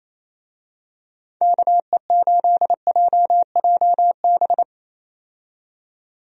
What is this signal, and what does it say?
Morse code, heard as a keyed tone, about 28 wpm, 700 Hz. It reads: KE8JJB
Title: USA Call Signs - 6 Characters - 28wpm